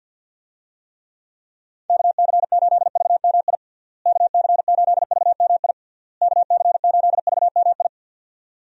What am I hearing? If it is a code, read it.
Morse code, heard as a keyed tone, about 52 wpm, 700 Hz. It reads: KC8VMI KC8VMI KC8VMI
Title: USA Call Signs - 6 Characters - Repeated 3x - 50wpm